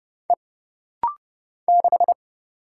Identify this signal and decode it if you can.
Morse code, heard as a keyed tone, about 30 wpm, 700 Hz. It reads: E 6
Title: Single Letter-Number Alt - Rapid-Fire 30wpm